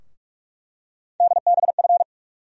Morse code, heard as a keyed tone, about 42 wpm, 700 Hz. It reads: DBF
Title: USA Call Sign Postfixes - 3 Letters 45wpm